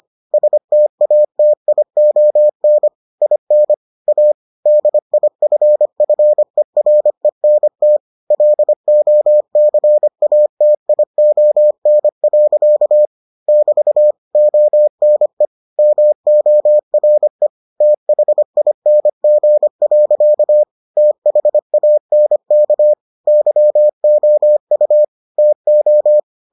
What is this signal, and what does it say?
Morse code, heard as a keyed tone, about 25 wpm, 600 Hz. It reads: STATION IN A DIFFERENT LOCATION. <BT> ONE MORE THING. THANK YOU TO